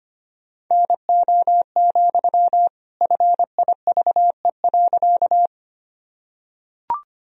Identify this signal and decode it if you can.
Morse code, heard as a keyed tone, about 25 wpm, 700 Hz. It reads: NO, FIVE.
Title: Sentences from Top 1000 Words - Review Part 2 - 25wpm